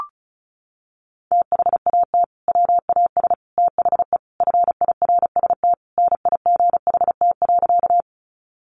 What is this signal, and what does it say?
Morse code, heard as a keyed tone, about 34 wpm, 700 Hz. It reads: THAT WAS THE FIRST NIGHT.
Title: Sentences from Top 300 Words 35wpm